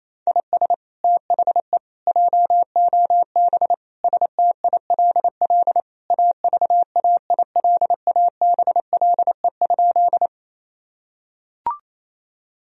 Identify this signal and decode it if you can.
Morse code, heard as a keyed tone, about 28 wpm, 700 Hz. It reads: IS THE JOB STILL AVAILABLE?
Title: Sentences from Top 1000 Words - Review Part 3 - 28wpm